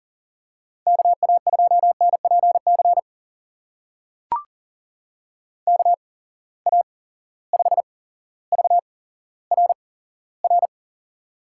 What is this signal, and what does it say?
Morse code, heard as a keyed tone, about 40 wpm, spaced as for 17 wpm, 700 Hz. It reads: KA2NPC KA5VRR
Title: USA Call Signs - 6 Characters - 8x Character Spacing - 40wpm